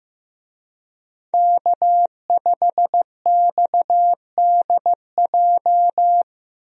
Morse code, heard as a keyed tone, about 15 wpm, 700 Hz. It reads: K5XDJ
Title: USA Call Signs - 5 Characters - 15wpm